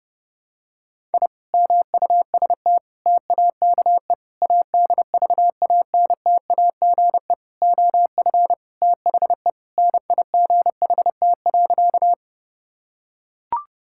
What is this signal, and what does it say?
Morse code, heard as a keyed tone, about 30 wpm, 700 Hz. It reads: I MUST TAKE ADVANTAGE OF THE NIGHT.
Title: Sentences from Top 1000 Words - Review Part 6 - 30wpm